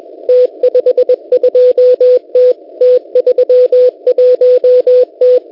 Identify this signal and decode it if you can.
Morse code, heard as a keyed tone, about 21 wpm, 500 Hz. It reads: T52T T31T
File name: Unid- -0701utc-4735khz